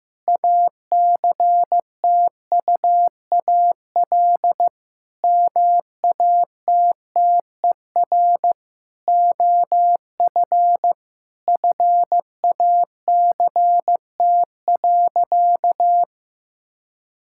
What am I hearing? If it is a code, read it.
Morse code, heard as a keyed tone, about 15 wpm, 700 Hz. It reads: ACTUAL MATTER OF FACT.
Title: Sentences from Top 1000 Words - Review Part 5 - 15wpm